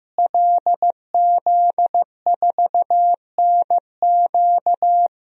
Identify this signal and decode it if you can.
Morse code, heard as a keyed tone, about 16 wpm, 700 Hz. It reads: LZ4NQ
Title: DXCC-100 Call Signs - 5 Characters - Repeated 3x - 15wpm